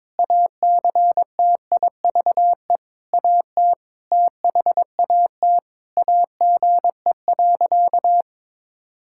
Morse code, heard as a keyed tone, about 22 wpm, 700 Hz. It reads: ACTIVE AT THAT AGE.
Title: Sentences from Top 1000 Words - Review Part 6 - 22wpm